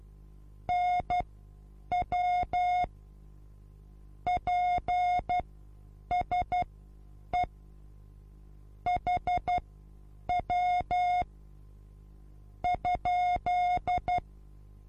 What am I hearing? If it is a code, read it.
Morse code, heard as a keyed tone, about 12 wpm, spaced as for 8 wpm, 720 Hz. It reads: NW PSE HW ?
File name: Onlinelehrgang / morsen / DJ4UF Morsen23